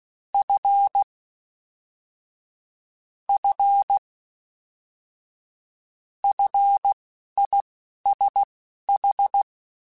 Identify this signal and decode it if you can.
Morse code, heard as a keyed tone, about 16 wpm, spaced as for 11 wpm, 800 Hz. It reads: F F FISH